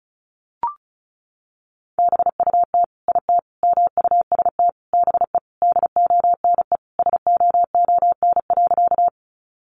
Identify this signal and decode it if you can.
Morse code, heard as a keyed tone, about 34 wpm, 700 Hz. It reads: BUT IT MUST BE DONE SOON.
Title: Sentences from Top 500 Words 35wpm